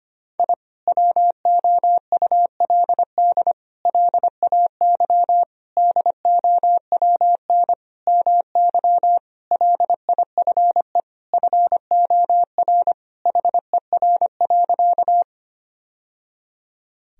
Morse code, heard as a keyed tone, about 25 wpm, 700 Hz. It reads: I WOULD LAY DOWN MY LIFE FOR HER.
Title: Sentences from Top 1000 Words - Review Part 2 - 25wpm